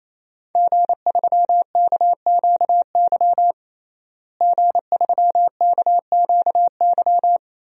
Morse code, heard as a keyed tone, about 28 wpm, 700 Hz. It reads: G3KQY G3KQY
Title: DXCC-100 Call Signs - 5 Characters - Repeated 3x - 28wpm